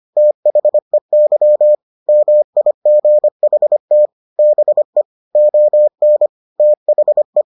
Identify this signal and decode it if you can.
Morse code, heard as a keyed tone, about 25 wpm, 600 Hz. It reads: THEY MIGHT BE ON THE